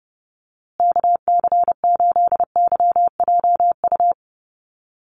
Morse code, heard as a keyed tone, about 30 wpm, 700 Hz. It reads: KC8YJU
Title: USA Call Signs Repeated 3x 30wpm